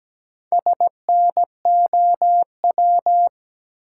Morse code, heard as a keyed tone, about 17 wpm, 700 Hz. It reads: SNOW
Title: Four Letter Words - 17wpm